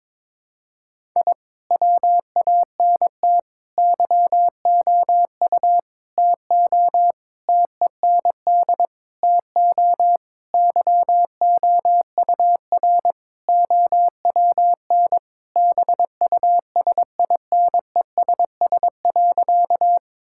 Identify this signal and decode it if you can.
Morse code, heard as a keyed tone, about 22 wpm, 700 Hz. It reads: I WANT YOU TO TEND TO YOUR OWN BUSINESS.
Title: Sentences from Top 1000 Words - Review Part 5 - 22wpm